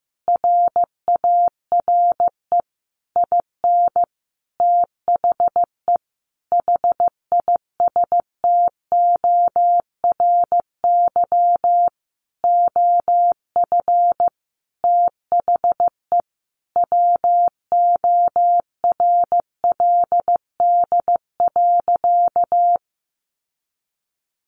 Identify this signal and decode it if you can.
Morse code, heard as a keyed tone, about 15 wpm, 700 Hz. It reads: RARE IN THE HISTORY OF THE WORLD.
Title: Sentences from Top 1100 Words 15wpm